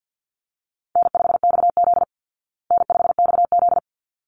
Morse code, heard as a keyed tone, about 50 wpm, 700 Hz. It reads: N5XZ N5XZ
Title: Easy Call Signs Repeated 2x - 50wpm